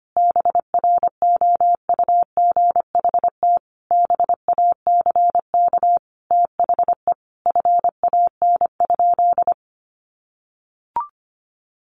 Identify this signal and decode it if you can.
Morse code, heard as a keyed tone, about 25 wpm, 700 Hz. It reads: BROUGHT BACK THE FAN?
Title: Sentences from Top 1300 Words 25wpm